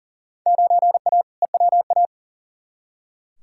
Morse code, heard as a keyed tone, about 40 wpm, 700 Hz. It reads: 9A EWA
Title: Field Day Exchanges Encore - 40wpm